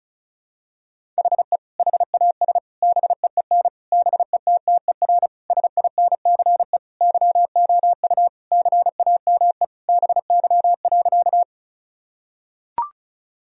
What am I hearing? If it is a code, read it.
Morse code, heard as a keyed tone, about 34 wpm, 700 Hz. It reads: HE HAS BEEN BETTER SINCE YOU CAME BY.
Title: Sentences from Top 1000 Words - Review Part 1 - 35wpm